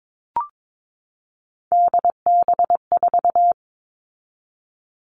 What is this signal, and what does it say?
Morse code, heard as a keyed tone, about 22 wpm, 700 Hz. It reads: DB4
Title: All Cut Numbers - 3-Digit 22wpm